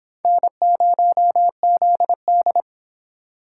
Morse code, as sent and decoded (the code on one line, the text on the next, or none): -. ----- --.. -..
N0ZD